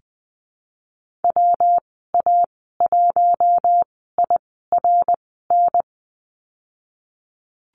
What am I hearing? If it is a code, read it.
Morse code, heard as a keyed tone, about 20 wpm, spaced as for 15 wpm, 700 Hz. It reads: WA1IRN
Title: 173 Lesson 28g - Number 9 - Call Signs 20wpm